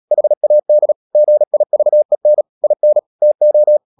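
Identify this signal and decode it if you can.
Morse code, heard as a keyed tone, about 37 wpm, 600 Hz. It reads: HAD GIVEN IN TO
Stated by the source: Morse code practice at 35–39 wpm sending English text of Ken Follett's novel Never, from CW med Gnister.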